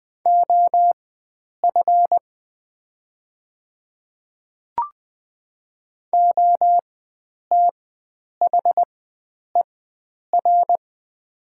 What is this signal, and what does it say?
Morse code, heard as a keyed tone, about 20 wpm, spaced as for 9 wpm, 700 Hz. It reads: OF OTHER
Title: 4x Character Spacing - Top 100 Words - 20wpm